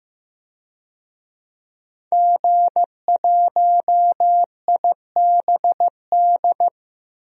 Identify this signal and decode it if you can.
Morse code, heard as a keyed tone, about 15 wpm, 700 Hz. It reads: G1IBD
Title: DXCC-100 Call Signs - 5 Characters - 15wpm